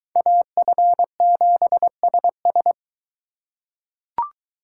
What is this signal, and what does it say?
Morse code, heard as a keyed tone, about 22 wpm, 700 Hz. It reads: AF7SS